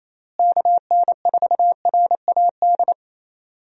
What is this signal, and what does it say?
Morse code, heard as a keyed tone, about 27 wpm, 700 Hz. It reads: KN4RAD
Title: USA Call Signs - 6 Characters - Repeated 2x - 28wpm